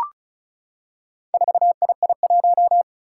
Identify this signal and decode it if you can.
Morse code, heard as a keyed tone, about 36 wpm, 700 Hz. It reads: 4II1
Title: Four Challenging Letter-Number Combinations 35wpm